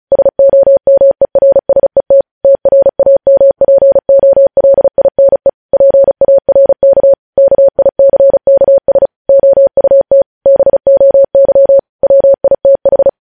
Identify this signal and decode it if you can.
Morse code, heard as a keyed tone, about 36 wpm, 550 Hz. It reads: SOMERSET TRAMPOLINE PARK KICKS OUT BOY WITH
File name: podcast / News headlines-1x-35x35